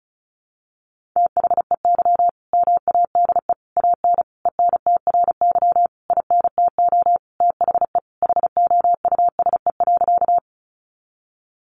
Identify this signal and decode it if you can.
Morse code, heard as a keyed tone, about 34 wpm, 700 Hz. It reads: THEY MADE AN ENTRY INTO THE HOUSE.
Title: Sentences from Top 1500 Words 35wpm